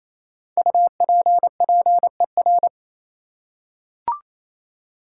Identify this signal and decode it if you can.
Morse code, heard as a keyed tone, about 28 wpm, 700 Hz. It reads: UPPER
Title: Five Letter Words - 28wpm